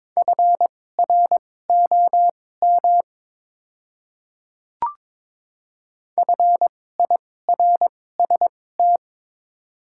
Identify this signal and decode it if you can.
Morse code, heard as a keyed tone, about 22 wpm, spaced as for 16 wpm, 700 Hz. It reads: FROM FIRST
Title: 2x Character Spacing - Top 100 Words - 22wpm